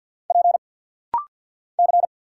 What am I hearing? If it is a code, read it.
Morse code, heard as a keyed tone, about 50 wpm, 700 Hz. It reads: P C
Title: Single Letters - Rapid-Fire 50wpm